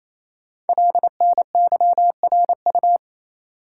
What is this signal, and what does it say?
Morse code, heard as a keyed tone, about 29 wpm, 700 Hz. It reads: LNYRU
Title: Code Groups - 5 Letter:Numbers - 28wpm